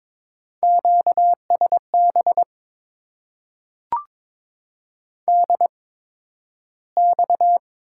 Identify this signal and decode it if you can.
Morse code, heard as a keyed tone, about 22 wpm, spaced as for 12 wpm, 690 Hz. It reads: QSB DX
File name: Top 90 QSO Elements - 8x Character Spacing - 22wpm